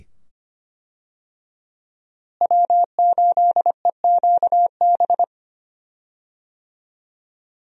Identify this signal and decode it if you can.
Morse code, heard as a keyed tone, about 25 wpm, 700 Hz. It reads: W8EQB
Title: USA Call Signs - 5 Characters - Repeated 2x - 25wpm